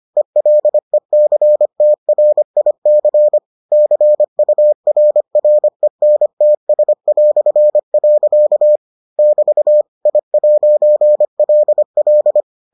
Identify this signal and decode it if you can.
Morse code, heard as a keyed tone, about 25 wpm, 600 Hz. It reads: ELECTRIC CURRENTS". <BT> I'LL